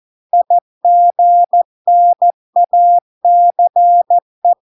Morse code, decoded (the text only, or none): IGNACE